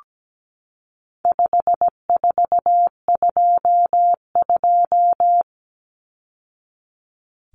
Morse code, heard as a keyed tone, about 17 wpm, 700 Hz. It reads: 5422
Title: Numbers - 4-Digit 17wpm